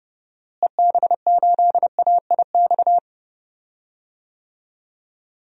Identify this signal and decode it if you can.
Morse code, heard as a keyed tone, about 30 wpm, 700 Hz. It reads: EB8AIX
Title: DXCC-100 Call Signs - 6 Characters - Repeated 3x - 30wpm